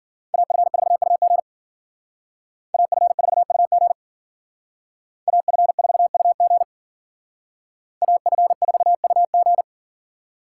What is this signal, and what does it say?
Morse code, heard as a keyed tone, about 50 wpm, 700 Hz. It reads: AF4UG AF4UG AF4UG AF4UG